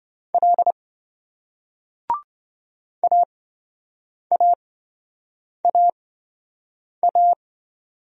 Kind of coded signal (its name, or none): Morse code